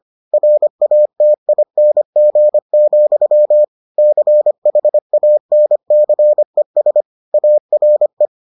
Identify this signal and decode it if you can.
Morse code, heard as a keyed tone, about 25 wpm, 600 Hz. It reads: RATING, CHANCES ARE